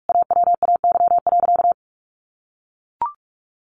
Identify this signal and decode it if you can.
Morse code, heard as a keyed tone, about 44 wpm, 700 Hz. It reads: AWAY.
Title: Sentences from Top 1500 Words 45wpm